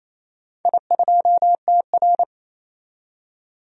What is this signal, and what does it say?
Morse code, heard as a keyed tone, about 28 wpm, 700 Hz. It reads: I2TR
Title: DXCC-100 Call Signs - 4 Characters - 28wpm